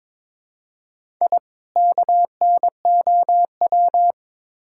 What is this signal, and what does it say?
Morse code, heard as a keyed tone, about 22 wpm, 700 Hz. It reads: I KNOW
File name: Build A Phrase - 2-Word Phrases - 22wpm